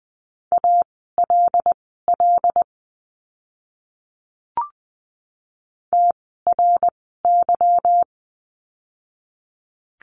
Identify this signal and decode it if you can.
Morse code, heard as a keyed tone, about 20 wpm, spaced as for 14 wpm, 700 Hz. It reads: ALL TRY
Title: 237 Lesson 38f - Letter q - Words 20wpm